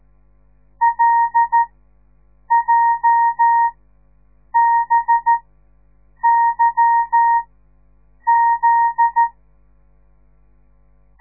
Morse code, heard as a keyed tone, about 13 wpm, spaced as for 7 wpm, 930 Hz. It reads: LJBYZ
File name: radiomorningstar / morse / les 9